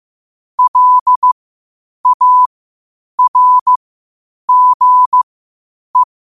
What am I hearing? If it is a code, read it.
Morse code, heard as a keyed tone, about 15 wpm, spaced as for 8 wpm, 1000 Hz. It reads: LARGE
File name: qotdfiles / 8wpm